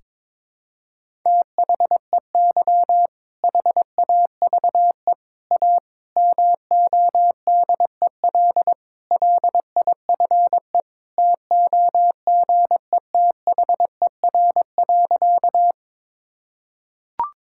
Morse code, heard as a keyed tone, about 22 wpm, 700 Hz. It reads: THEY HAVE A MODEL LIFE TOGETHER.
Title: Sentences from Top 1000 Words - Review Part 5 - 22wpm